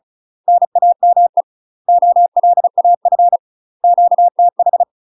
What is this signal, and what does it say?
Morse code, heard as a keyed tone, about 36 wpm, 700 Hz. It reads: NAME OLAF QTH
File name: UK MASTER QSOs TRAINING / 35WPM / UK MASTER QSOs 0017